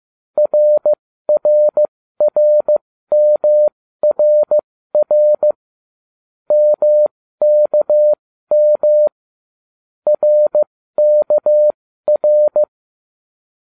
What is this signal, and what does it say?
Morse code, heard as a keyed tone, about 15 wpm, spaced as for 13 wpm, 600 Hz. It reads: RRRMRR MKM RKR